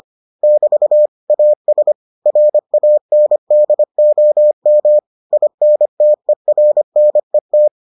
Morse code, heard as a keyed tone, about 25 wpm, 600 Hz. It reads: <BT> AS RANDOM INTERNET